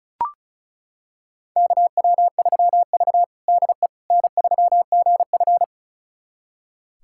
Morse code, heard as a keyed tone, about 36 wpm, 700 Hz. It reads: KW3V DE N3GF
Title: USA Callsign DE Callsign - 4 Characters - 35wpm